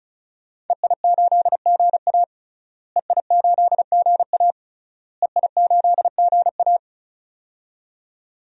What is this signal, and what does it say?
Morse code, heard as a keyed tone, about 34 wpm, 700 Hz. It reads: EI8GA EI8GA EI8GA
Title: DXCC-100 Call Signs - 5 Characters - Repeated 3x - 35wpm